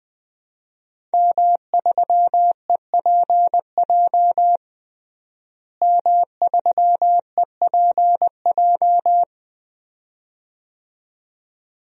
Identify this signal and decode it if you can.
Morse code, heard as a keyed tone, about 21 wpm, 700 Hz. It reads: M3EPJ M3EPJ
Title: DXCC-100 Call Signs - 5 Characters - Repeated 2x - 20wpm